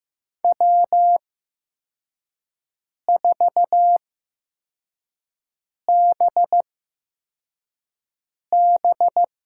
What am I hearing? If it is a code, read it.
Morse code, heard as a keyed tone, about 15 wpm, spaced as for 4 wpm, 700 Hz. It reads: W4BB
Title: USA Call Signs - 4 Characters - 8x Character Spacing - 15wpm